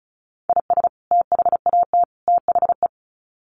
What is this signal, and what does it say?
Morse code, heard as a keyed tone, about 34 wpm, 700 Hz. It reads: IS THAT THE
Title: Sets of 3 Words 35wpm